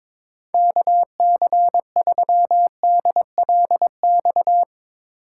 Morse code, heard as a keyed tone, about 22 wpm, 700 Hz. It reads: KC3DLX
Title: USA Call Signs - 6 Characters - 4x Character Spacing - 22wpm